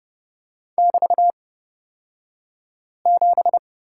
Morse code, heard as a keyed tone, about 31 wpm, spaced as for 5 wpm, 700 Hz. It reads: <BT>7
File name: Lesson 36d - BT - Two Characters 30wpm